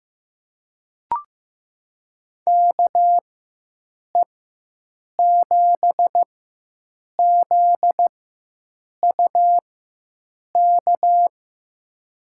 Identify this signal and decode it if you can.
Morse code, heard as a keyed tone, about 15 wpm, spaced as for 7 wpm, 700 Hz. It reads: KE7ZUK
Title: USA Call Signs - 6 Characters - 4x Character Spacing - 15wpm